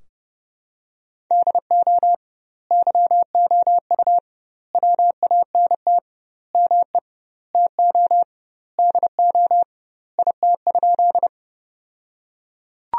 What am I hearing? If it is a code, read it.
Morse code, heard as a keyed tone, about 32 wpm, 700 Hz. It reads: DO YOU WANT ME TO DO IT?
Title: 2x Word-Spacing - Sentences from Top 200 Words - 30wpm